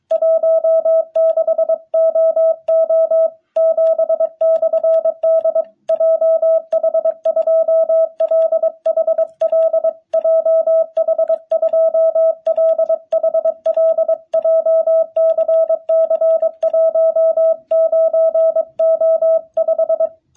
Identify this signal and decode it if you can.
Morse code, heard as a keyed tone, about 23 wpm, 660 Hz. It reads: 16OO7/D JH2LHL JH2LHL JCC 19O5
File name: jh8cbh / 2021cw